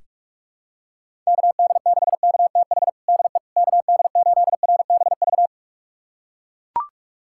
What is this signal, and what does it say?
Morse code, heard as a keyed tone, about 43 wpm, 700 Hz. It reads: KD6KTH DE KD8RBV
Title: USA Callsign DE Callsign - 6 Characters - 45wpm